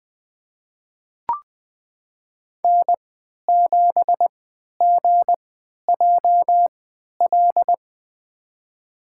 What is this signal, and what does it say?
Morse code, heard as a keyed tone, about 20 wpm, spaced as for 11 wpm, 700 Hz. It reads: N7GJL
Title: USA Call Signs - 5 Characters - 3x Character Spacing - 20wpm